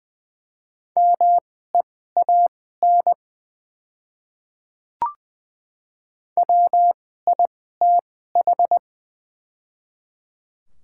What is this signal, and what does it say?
Morse code, heard as a keyed tone, about 20 wpm, spaced as for 14 wpm, 700 Hz. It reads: MEAN WITH
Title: 2x Character Spacing - Top 300 Words - 20wpm